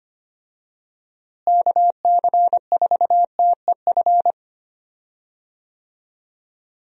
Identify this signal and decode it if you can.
Morse code, heard as a keyed tone, about 25 wpm, 700 Hz. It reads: KC4TEF
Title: USA Call Signs - 6 Characters - Repeated 2x - 25wpm